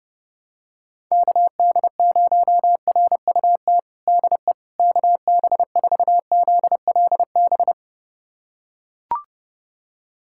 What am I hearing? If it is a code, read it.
Morse code, heard as a keyed tone, about 30 wpm, 700 Hz. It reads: KD0RUT DE KB4ZLB
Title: USA Callsign DE Callsign - 6 Characters - 30wpm